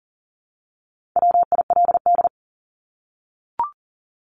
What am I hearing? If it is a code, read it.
Morse code, heard as a keyed tone, about 40 wpm, 700 Hz. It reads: WILD
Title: Top 1501-1750 Words 40wpm